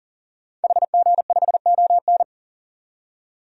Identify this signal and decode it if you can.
Morse code, heard as a keyed tone, about 39 wpm, 700 Hz. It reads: HG5ON
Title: DXCC-100 Call Signs - 5 Characters - 40wpm